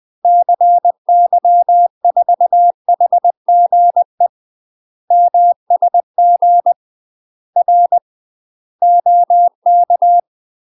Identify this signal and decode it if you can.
Morse code, heard as a keyed tone, about 20 wpm, 700 Hz. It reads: CY4HGE MSG R OK
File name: UK MASTER QSOs TRAINING/20WPM